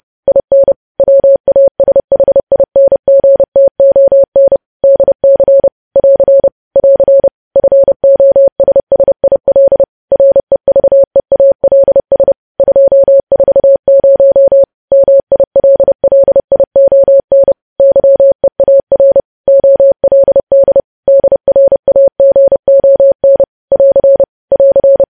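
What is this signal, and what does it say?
Morse code, heard as a keyed tone, about 30 wpm, 550 Hz. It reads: IN WASHINGTON DC <AR> <AR> FOSSIL REVEALS 240 MILLION YEAR OLD DRAGON <AR> <AR>
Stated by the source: Morse news headlines sent at 30 wpm, opening with vvv.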